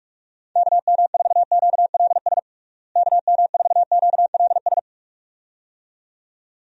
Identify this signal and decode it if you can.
Morse code, heard as a keyed tone, about 46 wpm, 700 Hz. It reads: KM4QLS KM4QLS
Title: USA Call Signs - 6 Characters - Repeated 2x - 45wpm